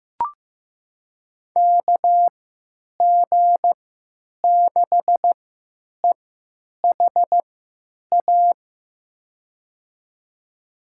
Morse code, heard as a keyed tone, about 15 wpm, spaced as for 9 wpm, 700 Hz. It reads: KG6EHA
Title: USA Call Signs - 6 Characters - 3x Character Spacing - 15wpm